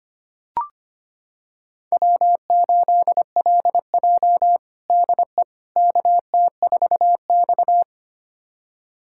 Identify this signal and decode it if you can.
Morse code, heard as a keyed tone, about 25 wpm, 700 Hz. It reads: W8LJ DE KT4X
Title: USA Callsign DE Callsign - 4 Characters - 25wpm